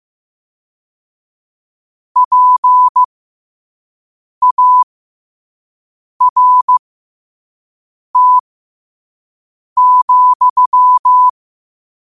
Morse code, heard as a keyed tone, about 15 wpm, spaced as for 5 wpm, 1000 Hz. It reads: PART,
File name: qotdfiles / 5wpm